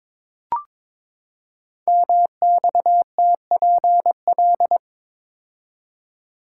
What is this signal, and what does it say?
Morse code, heard as a keyed tone, about 23 wpm, 700 Hz. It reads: MXTPL
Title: Code Groups - 5 Letters - 22wpm